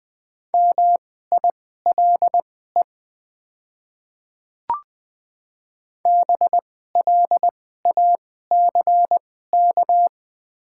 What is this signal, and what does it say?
Morse code, heard as a keyed tone, about 20 wpm, spaced as for 15 wpm, 690 Hz. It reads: MILE BLACK
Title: 2x Character Spacing - Top 500 Words - 20wpm